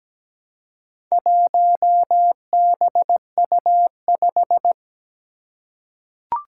Morse code, heard as a keyed tone, about 17 wpm, 700 Hz. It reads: 1BU5
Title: Four Challenging Letter-Number Combinations 17wpm